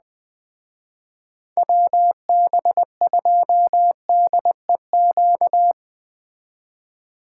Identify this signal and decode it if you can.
Morse code, heard as a keyed tone, about 21 wpm, 700 Hz. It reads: WB2DEQ